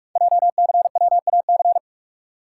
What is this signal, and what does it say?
Morse code, heard as a keyed tone, about 47 wpm, 700 Hz. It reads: JCWAC
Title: Code Groups - 5 Letters - 45wpm